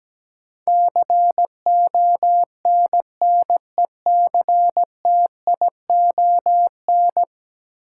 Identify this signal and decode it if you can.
Morse code, heard as a keyed tone, about 17 wpm, 700 Hz. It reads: CONNECTION
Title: Sentences from Top 1300 Words - New Word Drill - 17wpm